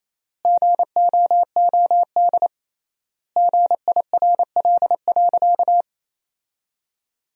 Morse code, heard as a keyed tone, about 28 wpm, 700 Hz. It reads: GOOD GIRL.
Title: 3x Word-Spacing - Sentences from Top 400 Words - 28wpm